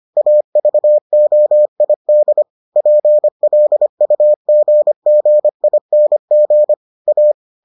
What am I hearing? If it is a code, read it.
Morse code, heard as a keyed tone, about 25 wpm, 600 Hz. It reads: AVOID PLUGGING A